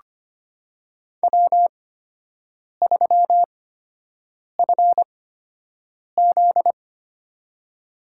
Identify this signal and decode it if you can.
Morse code, heard as a keyed tone, about 25 wpm, spaced as for 7 wpm, 700 Hz. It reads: W3FZ
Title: USA Call Signs - 4 Characters - 8x Character Spacing - 25wpm